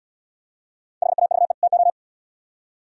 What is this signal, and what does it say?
Morse code, heard as a keyed tone, about 71 wpm, 700 Hz. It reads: HAVE TO